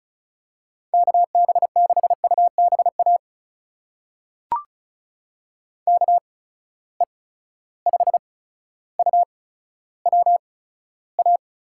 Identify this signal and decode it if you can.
Morse code, heard as a keyed tone, about 36 wpm, spaced as for 15 wpm, 700 Hz. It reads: KB6UBA KE5UWA
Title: USA Call Signs - 6 Characters - 8x Character Spacing - 35wpm